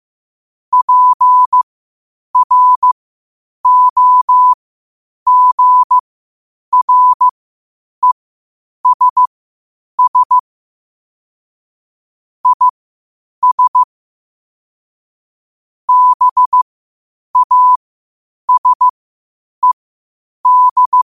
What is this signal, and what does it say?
Morse code, heard as a keyed tone, about 15 wpm, spaced as for 8 wpm, 1000 Hz. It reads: PROGRESS IS BASED
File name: qotdfiles / 8wpm